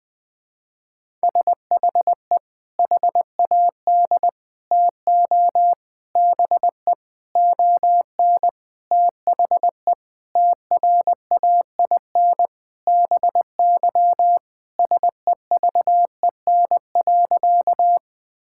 Morse code, heard as a keyed tone, about 20 wpm, 700 Hz. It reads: SHE HAD TO BE ON THE TRAIN BY SEVEN.
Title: Sentences with 4-Word Phrases - Spoken Beginning - 20wpm